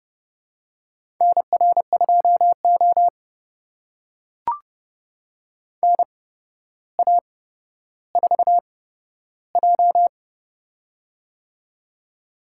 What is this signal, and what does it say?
Morse code, heard as a keyed tone, about 29 wpm, spaced as for 13 wpm, 700 Hz. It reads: NR2O NA4J
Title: USA Call Signs - 4 Characters - 8x Character Spacing - 30wpm